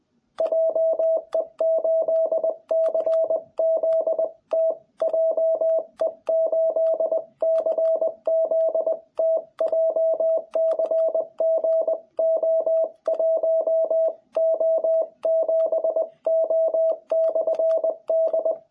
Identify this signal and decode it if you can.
Morse code, heard as a keyed tone, about 20 wpm, 660 Hz. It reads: JE8CZT JE8CZT JCGO1O7O/D
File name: jh8cbh / 2021cw